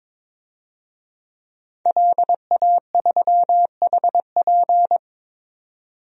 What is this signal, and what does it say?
Morse code, heard as a keyed tone, about 21 wpm, 700 Hz. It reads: LA3HP
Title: DXCC-100 Call Signs - 5 Characters - 22wpm